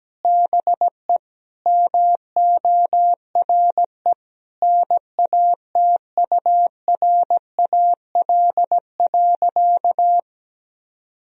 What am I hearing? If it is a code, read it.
Morse code, heard as a keyed tone, about 17 wpm, 700 Hz. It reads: BE MORE NATURAL.
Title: Sentences from Top 1000 Words - Review Part 4 - 17wpm